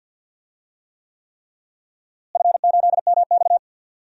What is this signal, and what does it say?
Morse code, heard as a keyed tone, about 51 wpm, 700 Hz. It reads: F8MX
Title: DXCC-100 Call Signs - 4 Characters - Repeated 3x - 50wpm